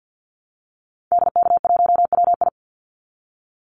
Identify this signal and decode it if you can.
Morse code, heard as a keyed tone, about 50 wpm, 700 Hz. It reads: DK1WI
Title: Call Signs - 50wpm